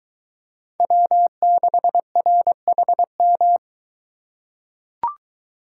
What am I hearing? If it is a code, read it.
Morse code, heard as a keyed tone, about 22 wpm, 700 Hz. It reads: W6RHM